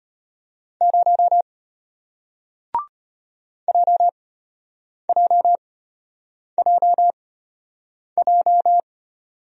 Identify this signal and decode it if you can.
Morse code, heard as a keyed tone, about 33 wpm, spaced as for 8 wpm, 700 Hz. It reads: 0 JJJJ